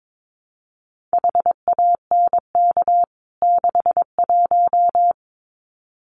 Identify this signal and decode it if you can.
Morse code, heard as a keyed tone, about 22 wpm, 700 Hz. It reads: HANK 61
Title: CWT Exchange 22wpm